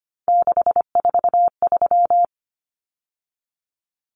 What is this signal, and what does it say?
Morse code, heard as a keyed tone, about 25 wpm, 700 Hz. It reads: 643
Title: Common Cut Numbers - 3-Digit 25wpm